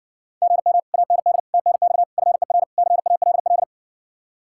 Morse code, heard as a keyed tone, about 60 wpm, 700 Hz. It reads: GM AND TNX FER CALL